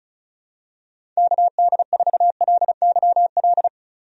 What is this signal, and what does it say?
Morse code, heard as a keyed tone, about 34 wpm, 700 Hz. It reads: KD4LYL
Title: USA Call Signs - 6 Characters - 35wpm